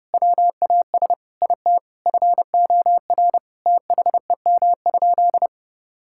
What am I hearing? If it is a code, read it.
Morse code, heard as a keyed tone, about 30 wpm, 700 Hz. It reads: WAS IT FOR THEM?
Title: Sentences from Top 1000 Words - Review Part 5 - 30wpm